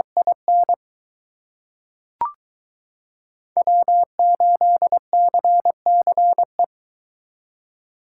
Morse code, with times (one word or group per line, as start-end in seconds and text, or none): 0.2-0.7 IN
3.6-6.6 W8CCE